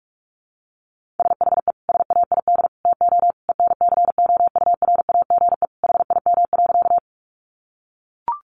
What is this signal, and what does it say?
Morse code, heard as a keyed tone, about 44 wpm, 700 Hz. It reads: SHE SAID TO ENCOURAGE HIM.
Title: Sentences from Top 1400 Words 45wpm